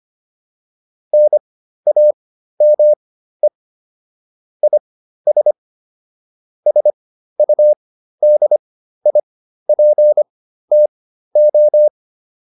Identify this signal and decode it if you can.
Morse code, heard as a keyed tone, about 25 wpm, spaced as for 13 wpm, 600 Hz. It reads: NAME IS SUDIPTO